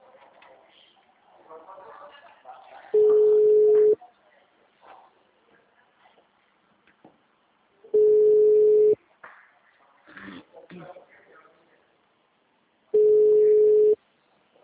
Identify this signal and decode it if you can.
Ringback tone (Europe, 425 Hz), with speech.
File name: out- -2006- - - .25592